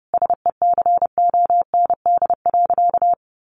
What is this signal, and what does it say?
Morse code, heard as a keyed tone, about 30 wpm, 700 Hz. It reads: SECOND.
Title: A Sound Of Thunder 30wpm